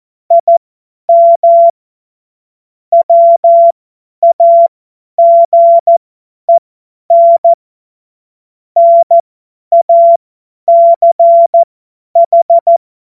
Morse code, heard as a keyed tone, about 14 wpm, spaced as for 10 wpm, 680 Hz. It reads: IM WAGEN NACH
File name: Morsekurs / Witze / Witz 17.Oktober 2023 50 F70